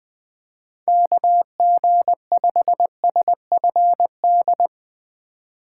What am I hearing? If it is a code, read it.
Morse code, heard as a keyed tone, about 20 wpm, 700 Hz. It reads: KG5SFD
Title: USA Call Signs - 6 Characters - 2x Character Spacing - 20wpm